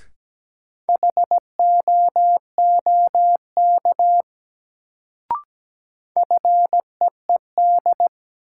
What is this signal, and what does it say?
Morse code, heard as a keyed tone, about 17 wpm, 700 Hz. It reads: HOOK FEED
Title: Sentences from Top 1600 Words - New Word Drill Alt - 17wpm